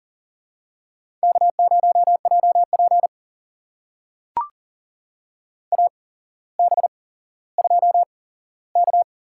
Morse code, heard as a keyed tone, about 40 wpm, spaced as for 17 wpm, 700 Hz. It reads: K0JP AB2K
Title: USA Call Signs - 4 Characters - 8x Character Spacing - 40wpm